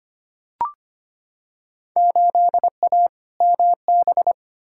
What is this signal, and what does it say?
Morse code, heard as a keyed tone, about 25 wpm, 700 Hz. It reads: 8A MB
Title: Field Day Exchanges Encore - 25wpm